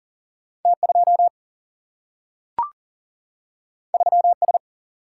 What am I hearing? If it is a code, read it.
Morse code, heard as a keyed tone, about 40 wpm, 700 Hz. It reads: T 2 3 S
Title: Single-Letter Single-Number 40wpm